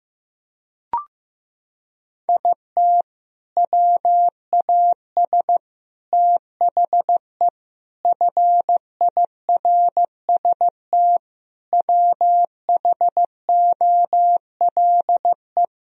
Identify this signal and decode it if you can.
Morse code, heard as a keyed tone, about 15 wpm, 700 Hz. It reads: IT WAS THE FIRST WHOLE
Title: Sentences from Top 1000 Words - Review Part 1 - 15wpm